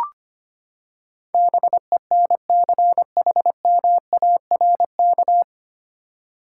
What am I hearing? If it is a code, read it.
Morse code, heard as a keyed tone, about 25 wpm, 700 Hz. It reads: BENCHMARK
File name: Nine Letter Words - 25wpm